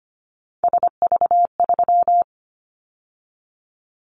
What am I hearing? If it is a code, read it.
Morse code, heard as a keyed tone, about 25 wpm, 690 Hz. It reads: SV3
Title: Callsign Prefixes for Top 50 DXCC - 25wpm